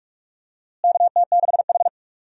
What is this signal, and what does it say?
Morse code, heard as a keyed tone, about 46 wpm, 700 Hz. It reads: KT6H
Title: USA Call Signs - 4 Characters - Repeated 2x - 45wpm